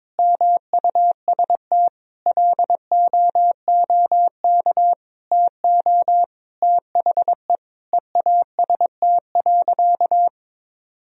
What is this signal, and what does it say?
Morse code, heard as a keyed tone, about 22 wpm, 700 Hz. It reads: MUST LOOK TO THE EAST.
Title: Sentences from Top 1000 Words - Review Part 3 - 22wpm